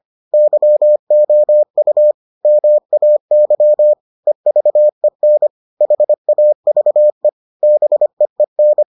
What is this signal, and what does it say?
Morse code, heard as a keyed tone, about 25 wpm, 600 Hz. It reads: YOU MAY EVEN HAVE BEEN